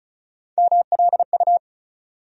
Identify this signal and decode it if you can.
Morse code, heard as a keyed tone, about 34 wpm, 700 Hz. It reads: MLU
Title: USA Call Sign Postfixes - 3 Letters 35wpm